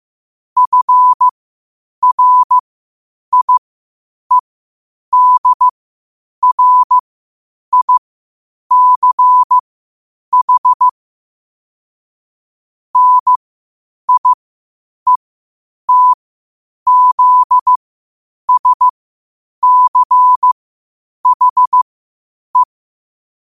Morse code, heard as a keyed tone, about 15 wpm, spaced as for 8 wpm, 1000 Hz. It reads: FRIEDRICH NIETZSCHE
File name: 8wpm